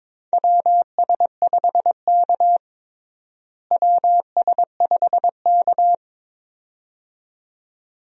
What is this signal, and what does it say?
Morse code, heard as a keyed tone, about 22 wpm, 700 Hz. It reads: WS5K WS5K
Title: USA Call Signs - 4 Characters - Repeated 2x - 22wpm